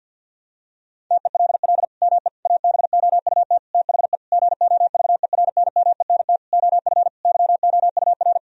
Morse code, heard as a keyed tone, about 52 wpm, 700 Hz. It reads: TELL ME ABOUT THE GOVERNMENT OF YOUR
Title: Sentences from Top 1000 Words - Review Part 3 - 50wpm